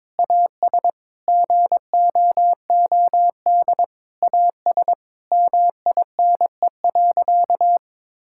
Morse code, heard as a keyed tone, about 22 wpm, 700 Hz. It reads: AS GOOD AS MINE.
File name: Sentences with 7-Word Phrases - 22wpm